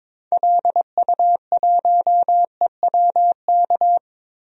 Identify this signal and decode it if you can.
Morse code, heard as a keyed tone, about 22 wpm, 700 Hz. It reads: LU1EWK
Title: DXCC-100 Call Signs - 6 Characters - Repeated 3x - 22wpm